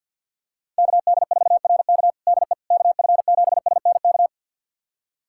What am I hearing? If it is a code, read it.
Morse code, heard as a keyed tone, about 50 wpm, 700 Hz. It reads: KD4RK DE KF7SNK
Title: USA Callsign DE Callsign 50wpm